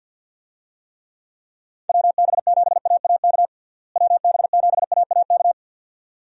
Morse code, heard as a keyed tone, about 52 wpm, 700 Hz. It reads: WB7AAK WB7AAK
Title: USA Call Signs - 6 Characters - Repeated 2x - 50wpm